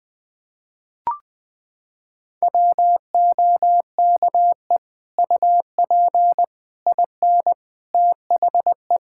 Morse code, heard as a keyed tone, about 20 wpm, 700 Hz. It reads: WOKE UP IN THE
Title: Sets of 8 Words - 20wpm